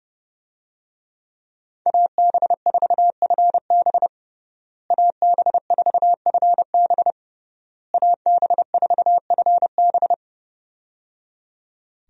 Morse code, heard as a keyed tone, about 30 wpm, 700 Hz. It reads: AB4FB AB4FB AB4FB
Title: USA Call Signs - 5 Characters - Repeated 3x - 30wpm